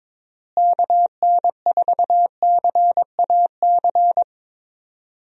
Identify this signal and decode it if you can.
Morse code, heard as a keyed tone, about 22 wpm, 700 Hz. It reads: KN4CAC
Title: USA Call Signs - 6 Characters - Repeated 2x - 22wpm